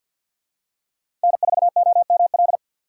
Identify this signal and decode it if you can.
Morse code, heard as a keyed tone, about 48 wpm, 700 Hz. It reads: N4OML
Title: USA Call Signs - 5 Characters - 6x Character Spacing - 50wpm